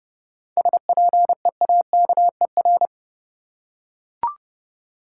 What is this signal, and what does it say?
Morse code, heard as a keyed tone, about 30 wpm, 700 Hz. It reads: SPEAKER
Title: Seven Letter Words - 30wpm